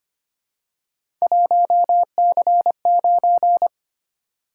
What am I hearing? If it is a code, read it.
Morse code, heard as a keyed tone, about 25 wpm, 700 Hz. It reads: 1C9
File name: Three Letter-Number Combinations - 25wpm